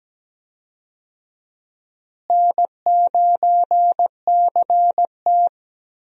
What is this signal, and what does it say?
Morse code, heard as a keyed tone, about 17 wpm, 700 Hz. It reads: N9CT
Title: USA Call Signs - 4 Characters - Repeated 3x - 17wpm